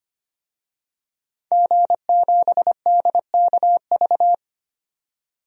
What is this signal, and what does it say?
Morse code, heard as a keyed tone, about 25 wpm, 700 Hz. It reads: G7DKV
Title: DXCC-100 Call Signs - 5 Characters - 25wpm